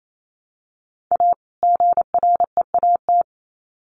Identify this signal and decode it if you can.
Morse code, heard as a keyed tone, about 28 wpm, 700 Hz. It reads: A GREAT
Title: Sets of 2 Words 28wpm